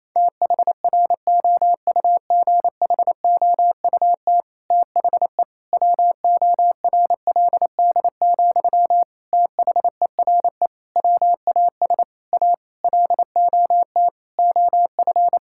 Morse code, heard as a keyed tone, about 28 wpm, 700 Hz. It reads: THROUGHOUT THE WORLD, THERE WAS A LOT OF
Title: Sentences from Top 1000 Words - Review Part 5 - 28wpm